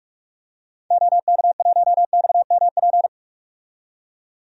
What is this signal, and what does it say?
Morse code, heard as a keyed tone, about 46 wpm, 700 Hz. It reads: OK1XMP
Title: DXCC-100 Call Signs - 6 Characters - 45wpm